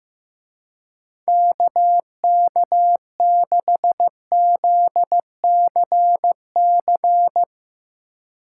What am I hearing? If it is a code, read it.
Morse code, heard as a keyed tone, about 15 wpm, 700 Hz. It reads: KK6ZCC
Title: Speed Racing - USA 2x3 Call Signs 15wpm